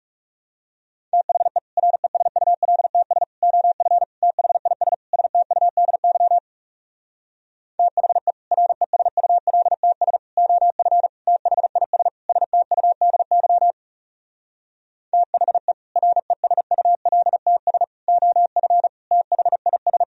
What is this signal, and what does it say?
Morse code, heard as a keyed tone, about 37 wpm, 700 Hz. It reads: THE RESULTS OF THIS STUDY THE RESULTS OF THIS STUDY THE RESULTS OF THIS
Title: Speed-Racing - Sets of 5 Words 30wpm